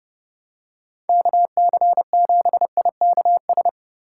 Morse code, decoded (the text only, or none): KC7IKS